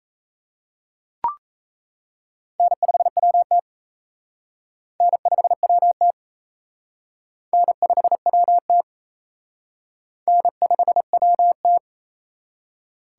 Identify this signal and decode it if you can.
Morse code, heard as a keyed tone, about 36 wpm, 700 Hz. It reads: N5WT N5WT N5WT N5WT